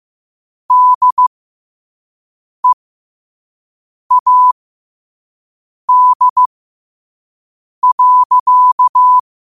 Morse code, heard as a keyed tone, about 15 wpm, spaced as for 5 wpm, 1000 Hz. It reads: DEAD.
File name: qotdfiles / 5wpm